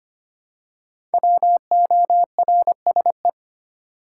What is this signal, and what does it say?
Morse code, heard as a keyed tone, about 25 wpm, 700 Hz. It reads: WORSE
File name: 5 Letter Words - First 3 Letters Given - 25wpm